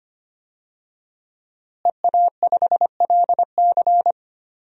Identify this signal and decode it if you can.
Morse code, heard as a keyed tone, about 25 wpm, 700 Hz. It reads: EA5LC
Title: DXCC-100 Call Signs - 5 Characters - Repeated 2x - 25wpm